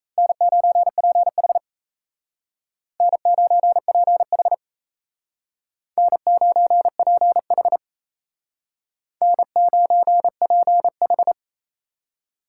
Morse code, heard as a keyed tone, about 36 wpm, 700 Hz. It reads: N9PH N9PH N9PH N9PH